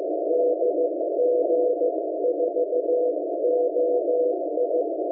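Morse code, heard as a keyed tone, about 15 wpm, 490 Hz, with noise. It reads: DG4OI
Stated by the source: SDR receiver Radioberry, Chinese-made transceiver from AliExpress, 14.058.67 MHz CW upper sideband, posted on RU-QRP Club forum.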